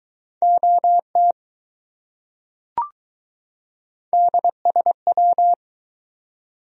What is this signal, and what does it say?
Morse code, heard as a keyed tone, about 20 wpm, 700 Hz. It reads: OT DSW